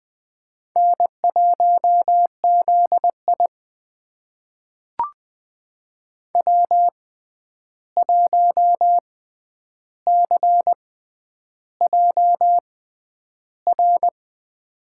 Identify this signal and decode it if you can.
Morse code, heard as a keyed tone, about 20 wpm, spaced as for 10 wpm, 700 Hz. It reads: N1ZI W1CJR
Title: USA Call Signs - 6x Character Spacing - 20wpm